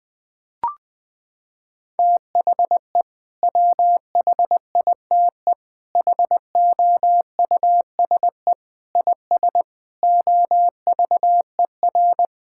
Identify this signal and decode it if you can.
Morse code, heard as a keyed tone, about 20 wpm, 700 Hz. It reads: THE WHITE HOUSE IS OVER
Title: Sentences with 3-Word Phrases - 20wpm